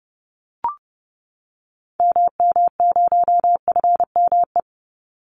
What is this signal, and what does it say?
Morse code, heard as a keyed tone, about 30 wpm, 700 Hz. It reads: MM0FME
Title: Call Signs - 30wpm